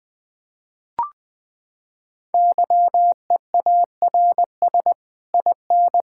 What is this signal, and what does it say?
Morse code, heard as a keyed tone, about 20 wpm, 700 Hz. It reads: YEARS IN
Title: Sets of 9 Words - 20wpm